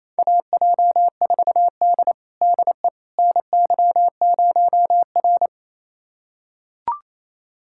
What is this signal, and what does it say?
Morse code, heard as a keyed tone, about 28 wpm, 700 Hz. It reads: AJ4D DE NY0R
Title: USA Callsign DE Callsign - 4 Characters - 28wpm